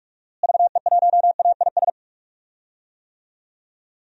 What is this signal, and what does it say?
Morse code, heard as a keyed tone, about 41 wpm, 700 Hz. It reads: VE1AIS
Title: DXCC-100 Call Signs - 6 Characters - Repeated 2x - 45wpm